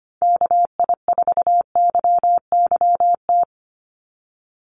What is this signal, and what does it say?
Morse code, heard as a keyed tone, about 25 wpm, 700 Hz. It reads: KI4YYT
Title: USA Call Signs 25wpm